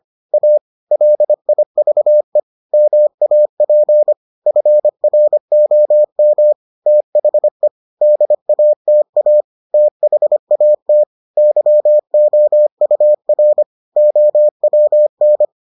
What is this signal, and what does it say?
Morse code, heard as a keyed tone, about 25 wpm, 600 Hz. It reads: A LIVE MAP FROM THE DATA THAT YOUR OWN